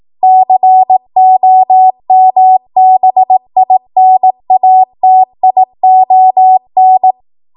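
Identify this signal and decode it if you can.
Morse code, heard as a keyed tone, about 18 wpm, 750 Hz. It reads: COMBINATION
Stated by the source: ARRL 18 wpm Morse code practice archive transmission.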